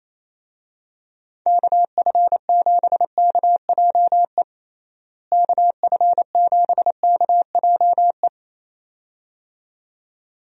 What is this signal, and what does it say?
Morse code, heard as a keyed tone, about 28 wpm, 700 Hz. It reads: KF7KJE KF7KJE
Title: USA Call Signs - 6 Characters - Repeated 2x - 28wpm